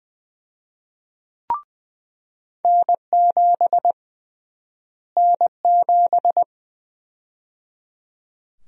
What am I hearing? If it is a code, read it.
Morse code, heard as a keyed tone, about 20 wpm, 700 Hz. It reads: N7 N7
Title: USA Call Sign Prefixes - Repeated 2x - 20wpm